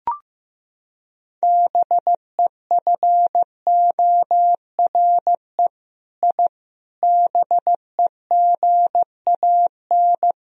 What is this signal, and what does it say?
Morse code, heard as a keyed tone, about 15 wpm, 700 Hz. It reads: BEFORE I BEGAN
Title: Sentences from Top 1000 Words - Review Part 4 - 15wpm